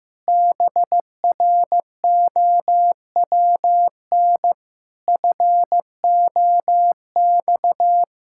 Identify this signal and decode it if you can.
Morse code, heard as a keyed tone, about 15 wpm, 700 Hz. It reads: BROWN FOX
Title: Pangram Challenge v2 - 15wpm